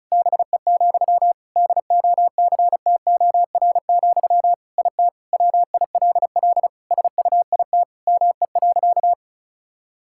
Morse code, heard as a keyed tone, about 34 wpm, 700 Hz. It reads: BE, DOCTOR, IT WILL SUIT ME.
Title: Sentences from Top 1600 Words 35wpm